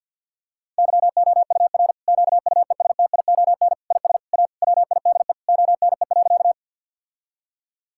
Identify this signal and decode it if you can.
Morse code, heard as a keyed tone, about 50 wpm, 700 Hz. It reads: YOUR QUESTION IS A WIDE ONE.
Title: Sentences from Top 1000 Words - Review Part 6 - 50wpm